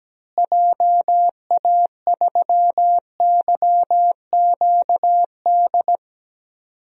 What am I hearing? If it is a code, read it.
Morse code, heard as a keyed tone, about 17 wpm, 700 Hz. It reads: JA3YQD
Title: DXCC-100 Call Signs - 6 Characters - Repeated 3x - 17wpm